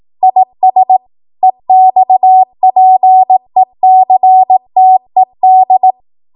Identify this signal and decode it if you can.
Morse code, heard as a keyed tone, about 18 wpm, 750 Hz. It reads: IS EXPECTED
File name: Morse / 18WPM